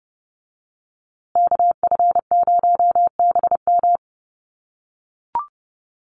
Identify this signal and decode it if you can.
Morse code, heard as a keyed tone, about 30 wpm, 700 Hz. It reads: KF0BM
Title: USA Call Signs Repeated 2x 30wpm